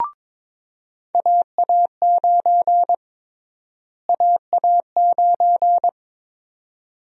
Morse code, heard as a keyed tone, about 22 wpm, 700 Hz. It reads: AA9 AA9
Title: USA Call Sign Prefixes - Repeated 2x - 22wpm